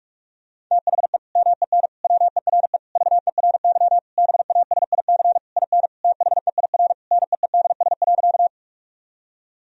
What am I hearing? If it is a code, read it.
Morse code, heard as a keyed tone, about 45 wpm, 700 Hz. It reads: THE MEN WERE VERY BASIC IN THEIR NEEDS.
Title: Sentences from Top 1000 Words - Review Part 3 - 45wpm